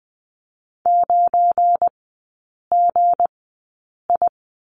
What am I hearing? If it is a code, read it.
Morse code, heard as a keyed tone, about 20 wpm, spaced as for 8 wpm, 700 Hz. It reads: 9GI
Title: 198 Lesson 32e - Letter k - Three Characters 20wpm